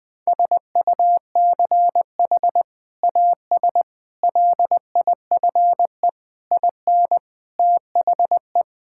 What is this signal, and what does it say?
Morse code, heard as a keyed tone, about 20 wpm, 700 Hz. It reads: SUCH AS LIFE IN THE
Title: Sentences from Top 1000 Words - Review Part 5 - 20wpm